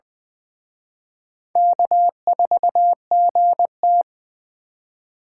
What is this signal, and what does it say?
Morse code, heard as a keyed tone, about 24 wpm, 700 Hz. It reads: K4GT
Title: Speed Racing - USA 1x2 Call Signs 20wpm